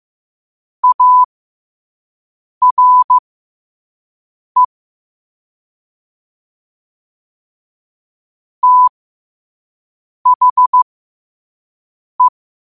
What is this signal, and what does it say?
Morse code, heard as a keyed tone, about 15 wpm, spaced as for 5 wpm, 1000 Hz. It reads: ARE THE
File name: qotdfiles / 5wpm